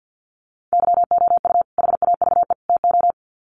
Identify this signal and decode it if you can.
Morse code, heard as a keyed tone, about 48 wpm, 700 Hz. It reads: YOU HAVE TO
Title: Sets of 3 Words 50wpm